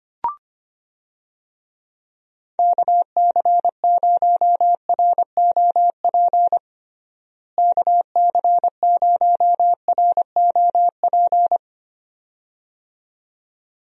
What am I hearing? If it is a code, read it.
Morse code, heard as a keyed tone, about 25 wpm, 700 Hz. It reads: KC0ROP KC0ROP
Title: USA Call Signs - 6 Characters - Repeated 2x - 25wpm